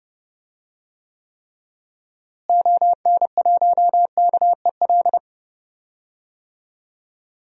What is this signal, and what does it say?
Morse code, heard as a keyed tone, about 30 wpm, 700 Hz. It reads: ON1KEL
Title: DXCC-100 Call Signs - 6 Characters - Repeated 3x - 30wpm